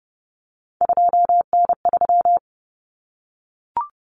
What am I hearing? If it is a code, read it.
Morse code, heard as a keyed tone, about 30 wpm, 700 Hz. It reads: 2N3
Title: Common Cut Numbers - 3-Digit 30wpm